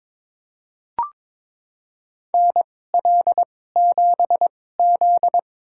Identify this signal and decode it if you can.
Morse code, heard as a keyed tone, about 22 wpm, spaced as for 16 wpm, 700 Hz. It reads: NL7Z
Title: USA Call Signs - 2x Character Spacing - 22wpm